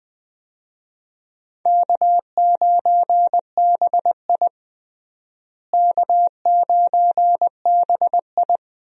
Morse code, heard as a keyed tone, about 20 wpm, 700 Hz. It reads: K9BI K9BI
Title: USA Call Signs - 4 Characters - Repeated 2x - 20wpm